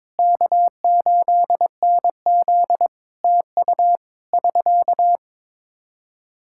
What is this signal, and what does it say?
Morse code, heard as a keyed tone, about 22 wpm, 700 Hz. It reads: K8NZ TU <SK>
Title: Practice QSO - Exchange 4 - Ending - 22wpm